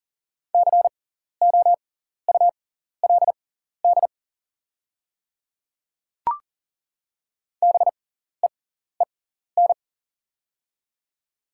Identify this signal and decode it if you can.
Morse code, heard as a keyed tone, about 40 wpm, spaced as for 14 wpm, 700 Hz. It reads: COULD BEEN
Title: 6x Character Spacing - Top 100 Words - 40wpm